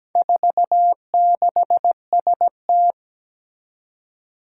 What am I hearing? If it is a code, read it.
Morse code, heard as a keyed tone, about 17 wpm, 700 Hz. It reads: 46ST
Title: Four Challenging Letter-Number Combinations 17wpm